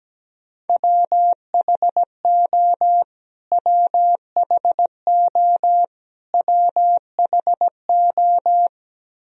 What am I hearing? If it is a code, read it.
Morse code, heard as a keyed tone, about 17 wpm, 700 Hz. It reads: WHO WHO WHO
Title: Top 100 Words Repeated 17wpm